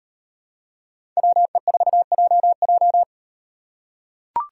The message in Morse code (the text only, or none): WE4JJ